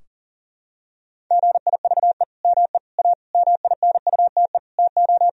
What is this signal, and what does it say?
Morse code, heard as a keyed tone, about 37 wpm, 700 Hz. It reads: GIVE ME A MINUTE TO
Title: Sentences from Top 1000 Words - Review Part 4 - 40wpm